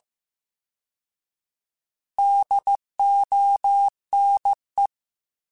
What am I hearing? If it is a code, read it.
Morse code, heard as a keyed tone, about 15 wpm, 800 Hz. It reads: DONE